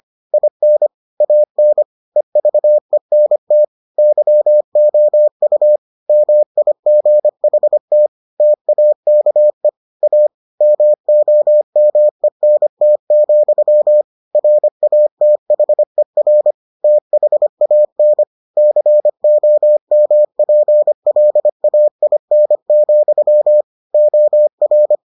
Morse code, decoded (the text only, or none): IN AN EVENT YOU MIGHT TAKE A MOMENT, RATHER THAN COMPLAIN, OR